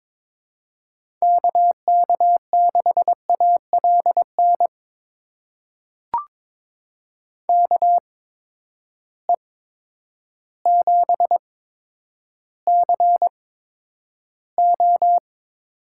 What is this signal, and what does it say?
Morse code, heard as a keyed tone, about 22 wpm, spaced as for 10 wpm, 700 Hz. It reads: KK6ALN KE7CO
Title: USA Call Signs - 6 Characters - 8x Character Spacing - 22wpm